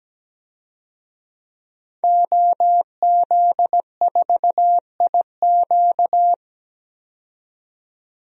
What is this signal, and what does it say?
Morse code, heard as a keyed tone, about 17 wpm, 700 Hz. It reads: OZ4IQ
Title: DXCC-100 Call Signs - 5 Characters - 17wpm